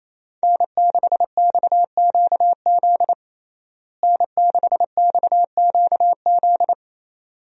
Morse code, heard as a keyed tone, about 28 wpm, 700 Hz. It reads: N6XQZ N6XQZ
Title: USA Call Signs - 5 Characters - Repeated 2x - 28wpm